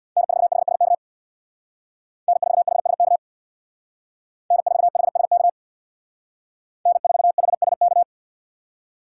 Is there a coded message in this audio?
Yes: Morse code from 0.2 to 8.0 s, heard as a keyed tone, about 64 wpm, 700 Hz. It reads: N4HSK N4HSK N4HSK N4HSK